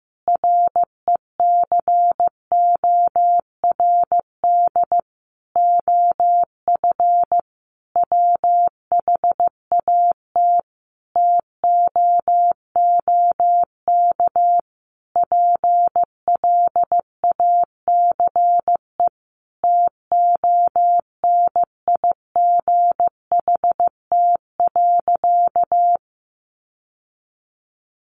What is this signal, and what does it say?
Morse code, heard as a keyed tone, about 15 wpm, 700 Hz. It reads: RECORD OF WHAT TOOK PLACE TONIGHT.
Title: Sentences from Top 1300 Words 15wpm